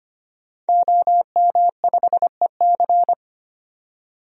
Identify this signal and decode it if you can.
Morse code, heard as a keyed tone, about 25 wpm, 700 Hz. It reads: OM5EC
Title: DXCC-100 Call Signs - 5 Characters - 25wpm